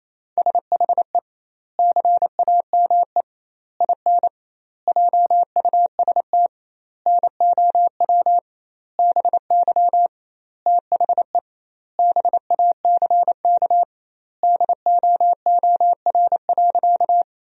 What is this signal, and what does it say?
Morse code, heard as a keyed tone, about 27 wpm, 700 Hz. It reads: SHE CAME IN JUST NOW BY THE BACK DOOR.
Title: 2x Word-Spacing - Sentences from Top 300 Words - 28wpm